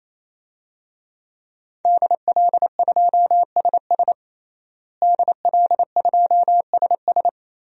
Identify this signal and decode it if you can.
Morse code, heard as a keyed tone, about 28 wpm, 700 Hz. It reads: DL2SS DL2SS
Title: DXCC-100 Call Signs - 5 Characters - Repeated 3x - 28wpm